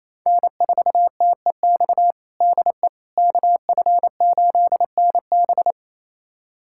Morse code, heard as a keyed tone, about 28 wpm, 700 Hz. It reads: N4TEX DE KF8NB
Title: USA Callsign DE Callsign - 5 Characters - 28wpm